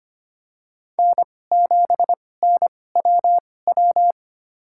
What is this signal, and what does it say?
Morse code, heard as a keyed tone, about 25 wpm, spaced as for 18 wpm, 700 Hz. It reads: N7NWW
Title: USA Call Signs - 5 Characters - 2x Character Spacing - 25wpm